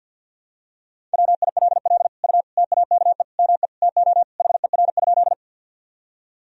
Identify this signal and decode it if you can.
Morse code, heard as a keyed tone, about 46 wpm, 700 Hz. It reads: WILL U TAKE ME TO HER?
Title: Sentences from Top 200 Words and Abbreviations 50wpm